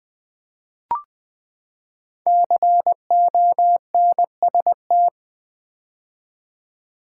Morse code, heard as a keyed tone, about 20 wpm, 700 Hz. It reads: CONST
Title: Build A Word - 5 Letters At A Time - 12-Letter Words - 20wpm